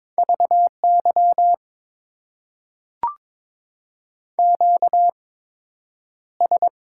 Morse code, heard as a keyed tone, about 22 wpm, spaced as for 9 wpm, 700 Hz. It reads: VY QS
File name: Top 90 QSO Elements - 8x Character Spacing - 22wpm